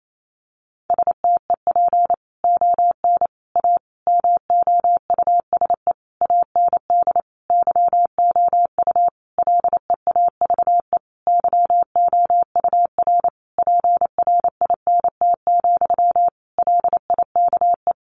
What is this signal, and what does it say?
Morse code, heard as a keyed tone, about 28 wpm, 700 Hz. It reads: STEP ON A MOUSE AND YOU LEAVE YOUR PRINT, LIKE
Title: A Sound Of Thunder 28wpm